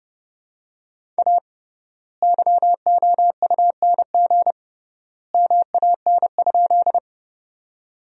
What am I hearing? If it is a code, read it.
Morse code, heard as a keyed tone, about 30 wpm, 700 Hz. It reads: A YOUNG MAN?
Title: 3x Word-Spacing - Sentences from Top 400 Words - 30wpm